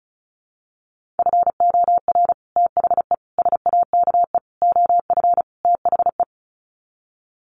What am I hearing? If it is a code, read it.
Morse code, heard as a keyed tone, about 34 wpm, 700 Hz. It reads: FOR THE SAKE OF THE
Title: Sets of 5 Words Encore 35wpm